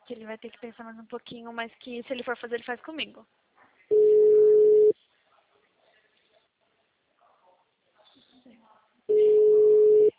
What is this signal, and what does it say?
Ringback tone (Europe, 425 Hz), with speech.